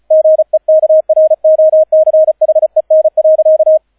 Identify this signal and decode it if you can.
Morse code, heard as a keyed tone, about 36 wpm, 630 Hz, with radio noise. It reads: GEKROCHEN.